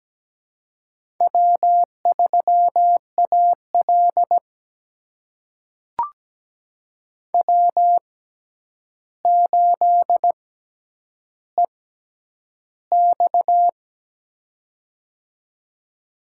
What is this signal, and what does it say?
Morse code, heard as a keyed tone, about 17 wpm, spaced as for 9 wpm, 700 Hz. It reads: W3AL W8EX
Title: USA Call Signs - 4 Characters - 6x Character Spacing - 17wpm